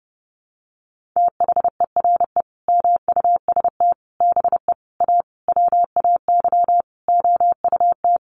Morse code, heard as a keyed tone, about 30 wpm, 700 Hz. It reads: THERE MUST BE A WAY OUT
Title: Sentences from Top 1400 Words 30wpm